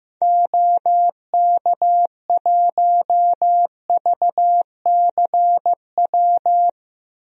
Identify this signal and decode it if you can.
Morse code, heard as a keyed tone, about 15 wpm, 700 Hz. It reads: OK1VCW
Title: DXCC-100 Call Signs - 6 Characters - Repeated 3x - 15wpm